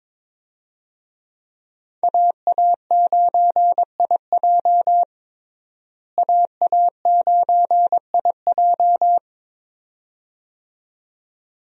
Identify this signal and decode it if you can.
Morse code, heard as a keyed tone, about 22 wpm, 700 Hz. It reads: AA9IJ AA9IJ
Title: USA Call Signs - 5 Characters - Repeated 2x - 22wpm